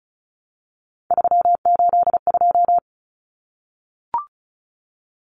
Morse code, heard as a keyed tone, about 34 wpm, 700 Hz. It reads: 382
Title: Common Cut Numbers - 3-Digit 35wpm